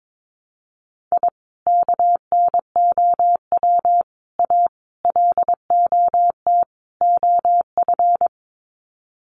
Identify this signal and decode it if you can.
Morse code, heard as a keyed tone, about 22 wpm, 700 Hz. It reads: I KNOW A LOT OF
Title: Sets of 5 Words Encore 22wpm